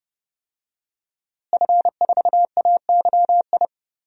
Morse code, heard as a keyed tone, about 30 wpm, 700 Hz. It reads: F4AYI
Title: DXCC-100 Call Signs - 5 Characters - Repeated 2x - 30wpm